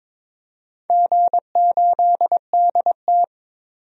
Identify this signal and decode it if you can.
Morse code, heard as a keyed tone, about 22 wpm, 700 Hz. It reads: G8DT
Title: DXCC-100 Call Signs - 4 Characters - 22wpm